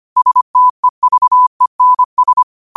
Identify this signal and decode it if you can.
Morse code, heard as a keyed tone, about 25 wpm, 1000 Hz. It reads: STEVENS
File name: qotdfiles / 25wpm